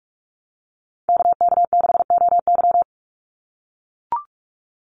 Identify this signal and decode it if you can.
Morse code, heard as a keyed tone, about 44 wpm, 700 Hz. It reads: KK6OY
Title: USA Call Signs 45wpm